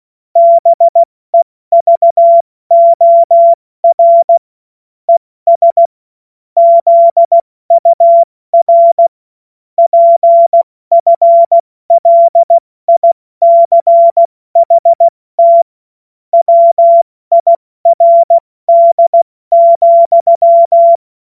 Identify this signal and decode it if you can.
Morse code, heard as a keyed tone, about 16 wpm, 680 Hz. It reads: BEVOR ES ZUR PFLICHT WIRD,
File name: Morsekurs / Witze / Witz 19. April 2022 70 F80